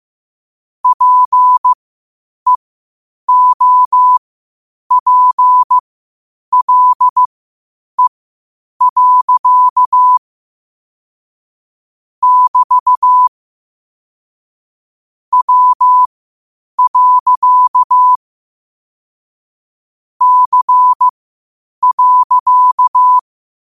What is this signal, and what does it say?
Morse code, heard as a keyed tone, about 15 wpm, spaced as for 8 wpm, 1000 Hz. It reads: PEOPLE. <BT> W. C.
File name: qotdfiles / 8wpm